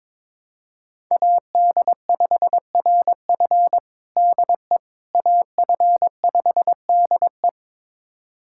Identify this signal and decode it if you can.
Morse code, heard as a keyed tone, about 21 wpm, 700 Hz. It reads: AD5RF DE AF5DE
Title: USA Callsign DE Callsign - 5 Characters - 22wpm